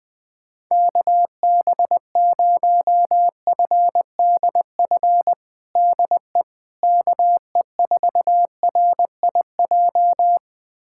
Morse code, heard as a keyed tone, about 20 wpm, 700 Hz. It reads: KB0FDF DE KE4RIJ
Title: USA Callsign DE Callsign 20wpm